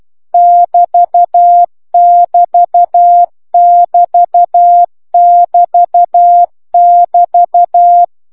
Morse code, heard as a keyed tone, about 12 wpm, 1700 Hz, with noise. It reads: <BT>DT<BT><BT><BT>
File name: cw egyen